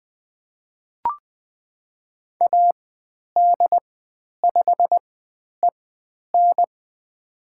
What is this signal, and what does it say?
Morse code, heard as a keyed tone, about 20 wpm, spaced as for 10 wpm, 700 Hz. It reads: AD5EN
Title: 102 Lesson 17g - Letter u - Call Signs 20wpm